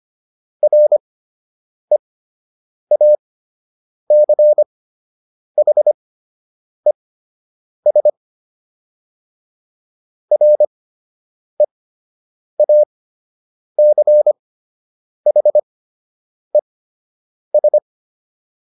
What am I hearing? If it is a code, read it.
Morse code, heard as a keyed tone, about 25 wpm, spaced as for 8 wpm, 600 Hz. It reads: REACHES REACHES